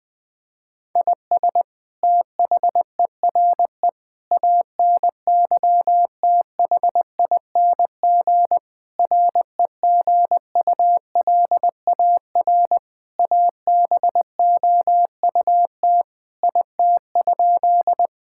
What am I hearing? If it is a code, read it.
Morse code, heard as a keyed tone, about 20 wpm, 700 Hz. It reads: IS THERE ANYTHING REGULAR ABOUT IT?
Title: Sentences from Top 1000 Words - Review Part 6 - 20wpm